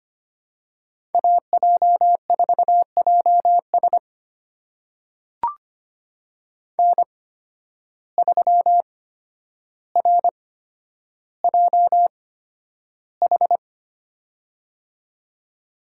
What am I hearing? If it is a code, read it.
Morse code, heard as a keyed tone, about 25 wpm, spaced as for 11 wpm, 700 Hz. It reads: AJ4JS N3RJH
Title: USA Call Signs - 5 Characters - 8x Character Spacing - 25wpm